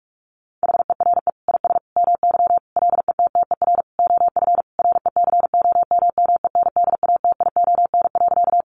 Morse code, heard as a keyed tone, about 44 wpm, 700 Hz. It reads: HERE IS MY LETTER OF RECOMMENDATION.
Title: Sentences from Top 1400 Words 45wpm